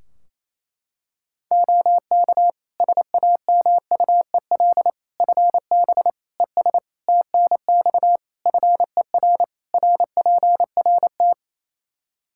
Morse code, heard as a keyed tone, about 28 wpm, 700 Hz. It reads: OK SAMUEL FB ES TNX FER RPRT
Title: K1USN SST QSO Practice 28wpm